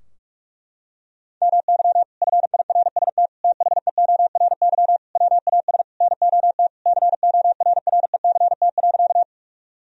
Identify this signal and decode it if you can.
Morse code, heard as a keyed tone, about 46 wpm, 700 Hz. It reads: MY FIRST THEORY WAS NOT CORRECT.
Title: Sentences from Top 1000 Words - Review Part 3 - 45wpm